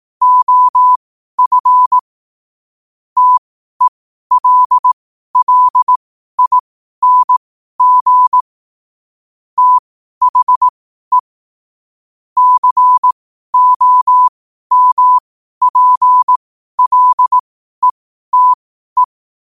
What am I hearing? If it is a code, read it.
Morse code, heard as a keyed tone, about 18 wpm, spaced as for 12 wpm, 1000 Hz. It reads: OF TELLING THE COMPLETE
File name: qotdfiles / 12wpm